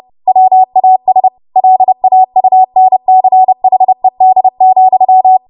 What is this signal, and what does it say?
Morse code, heard as a keyed tone, about 30 wpm, 750 Hz. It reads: WAS LAUNCHED,
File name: Archive / 30 WPM / 30WPM